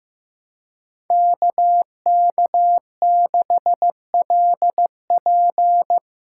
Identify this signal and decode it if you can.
Morse code, heard as a keyed tone, about 16 wpm, 700 Hz. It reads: KK6LP
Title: USA Callsign DE Callsign - 5 Characters - 15wpm